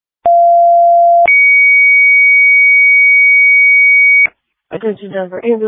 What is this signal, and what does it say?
Two-tone page, heard as a one-second tone, then a three-second tone, about 690 then 2250 Hz, with speech. It is heard from 0.2 to 4.3 s.